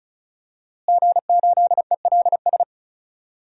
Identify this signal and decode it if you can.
Morse code, heard as a keyed tone, about 36 wpm, 700 Hz. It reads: G8ELS
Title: DXCC-100 Call Signs - 5 Characters - Repeated 2x - 35wpm